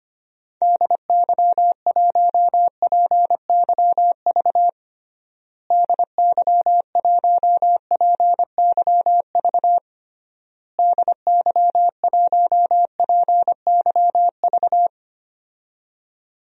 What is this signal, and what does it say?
Morse code, heard as a keyed tone, about 25 wpm, 700 Hz. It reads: DY1PYV DY1PYV DY1PYV
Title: DXCC-100 Call Signs - 6 Characters - Repeated 3x - 25wpm